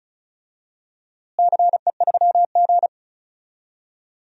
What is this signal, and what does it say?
Morse code, heard as a keyed tone, about 36 wpm, 700 Hz. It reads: CE3G
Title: DXCC-100 Call Signs - 4 Characters - Repeated 2x - 35wpm